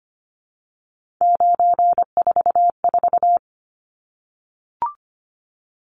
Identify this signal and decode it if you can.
Morse code, heard as a keyed tone, about 25 wpm, 700 Hz. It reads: 944
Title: Numbers - 3-Digit 25wpm